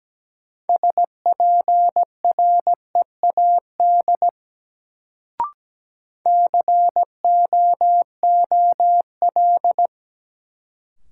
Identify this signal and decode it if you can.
Morse code, heard as a keyed tone, about 17 wpm, 700 Hz. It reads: SPREAD COOL
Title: Sentences from Top 1200 Words - New Word Drill Alt - 17wpm